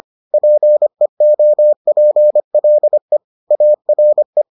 Morse code, heard as a keyed tone, about 25 wpm, 600 Hz. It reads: PEOPLE ARE